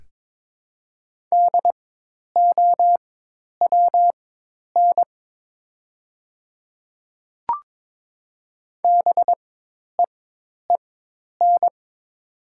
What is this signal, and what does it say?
Morse code, heard as a keyed tone, about 22 wpm, spaced as for 10 wpm, 700 Hz. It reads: DOWN BEEN
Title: 4x Character Spacing - Top 100 Words - 22wpm